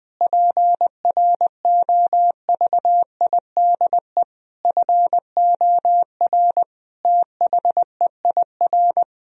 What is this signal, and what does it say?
Morse code, heard as a keyed tone, about 20 wpm, 700 Hz. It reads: PROVIDE FOR THEIR
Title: Sentences from Top 1000 Words - Review Part 3 - 20wpm